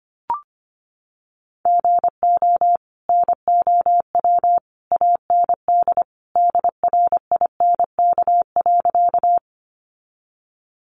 Morse code, heard as a keyed tone, about 25 wpm, 700 Hz. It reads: GO NOW AND DRINK.
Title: Sentences from Top 1500 Words 25wpm